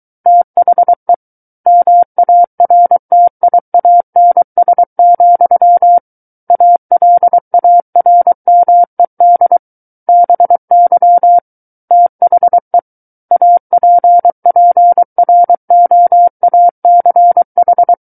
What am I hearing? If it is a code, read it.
Morse code, heard as a keyed tone, about 23 wpm, 700 Hz. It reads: THE MARTIANS, ALARMED BY THE APPROACH
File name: Book One / 14-Chapter 14 - 23 WPM